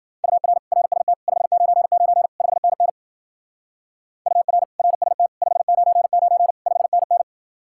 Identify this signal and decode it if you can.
Morse code, heard as a keyed tone, about 49 wpm, 700 Hz. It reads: UR RST 599 5NN UR RST 599 5NN
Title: Speed-Racing Common QSO Phrases Encore - 40wpm